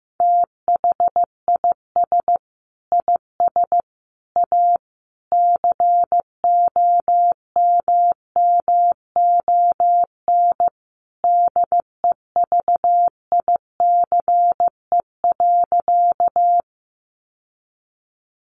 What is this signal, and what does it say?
Morse code, heard as a keyed tone, about 15 wpm, 700 Hz. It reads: THIS IS A COMMON DEVICE.
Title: Sentences from Top 1500 Words 15wpm